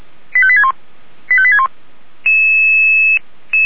Five-tone selective call (ZVEI), heard as quick runs of five tones, tones of about 70 ms. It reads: 85861 85861